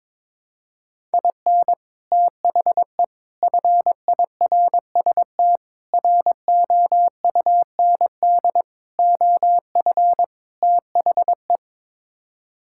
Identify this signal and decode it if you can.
Morse code, heard as a keyed tone, about 22 wpm, 700 Hz. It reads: IN THE FIRST ROUND OF THE
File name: practice / 22wpm / Sets of 6 Words - 22wpm